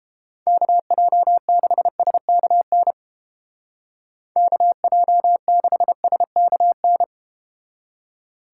Morse code, heard as a keyed tone, about 32 wpm, 700 Hz. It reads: KJ6SKN KJ6SKN